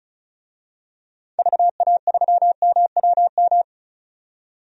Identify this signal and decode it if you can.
Morse code, heard as a keyed tone, about 34 wpm, 700 Hz. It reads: VA3MWM
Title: DXCC-100 Call Signs - 6 Characters - 35wpm